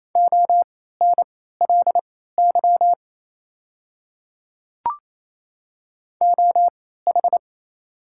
Morse code, heard as a keyed tone, about 28 wpm, spaced as for 16 wpm, 700 Hz. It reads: ONLY OH
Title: 3x Character Spacing - Top 500 Words - 28wpm